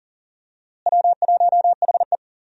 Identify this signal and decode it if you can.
Morse code, heard as a keyed tone, about 40 wpm, 700 Hz. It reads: W1HE
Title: USA Call Signs - 4 Characters - Repeated 3x - 40wpm